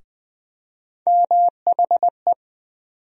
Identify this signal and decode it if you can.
Morse code, heard as a keyed tone, about 20 wpm, 700 Hz. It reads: MHE
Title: USA Call Sign Postfixes - 3 Letters 20wpm